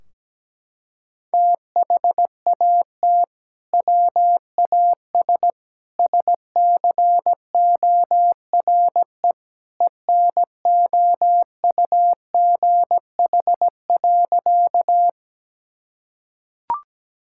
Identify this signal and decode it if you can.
Morse code, heard as a keyed tone, about 17 wpm, 700 Hz. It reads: THAT WAS SCORE ENOUGH.
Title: Sentences from Top 1600 Words 17wpm